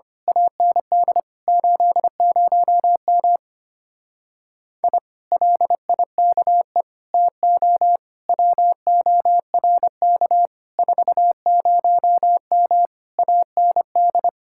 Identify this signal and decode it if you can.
Morse code, heard as a keyed tone, about 24 wpm, 700 Hz. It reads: AND 80M I LIKE TO WORK 40M AND
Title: Speed-Racing Common QSO Phrases - 25wpm